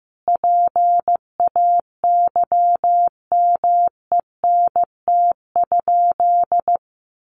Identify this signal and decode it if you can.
Morse code, heard as a keyed tone, about 15 wpm, 700 Hz. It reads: PAYMENT?
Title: Sentences from Top 1200 Words 15wpm